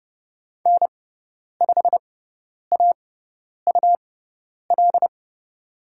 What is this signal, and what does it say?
Morse code, heard as a keyed tone, about 31 wpm, spaced as for 10 wpm, 700 Hz. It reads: N5AUL
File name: Lesson 18g - Letter m - Call Signs 30wpm